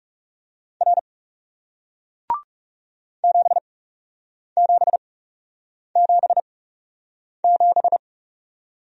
Morse code, heard as a keyed tone, about 41 wpm, spaced as for 8 wpm, 700 Hz. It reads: R 7777